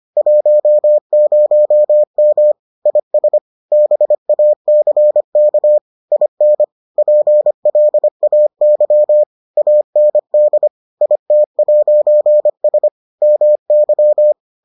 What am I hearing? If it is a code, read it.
Morse code, heard as a keyed tone, about 25 wpm, 600 Hz. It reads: 10M IS BACK IN PLAY AND IT'S MY